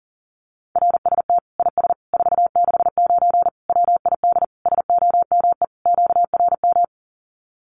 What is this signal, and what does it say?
Morse code, heard as a keyed tone, about 40 wpm, 700 Hz. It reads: RST IS 469 WID SOME QRM
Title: Common QSO Phrases 40wpm